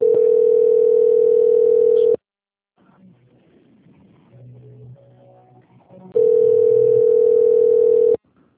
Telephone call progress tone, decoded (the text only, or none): ringback tone